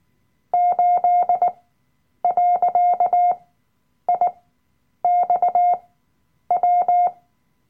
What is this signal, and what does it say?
Morse code, heard as a keyed tone, about 19 wpm, spaced as for 9 wpm, 710 Hz. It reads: 8.IXW
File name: Workshop Dateien / CW-Online-Kurs / Lektion 007